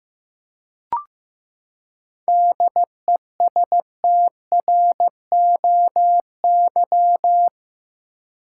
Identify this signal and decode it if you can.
Morse code, heard as a keyed tone, about 15 wpm, 700 Hz. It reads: DESTROY
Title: Seven Letter Words - 15wpm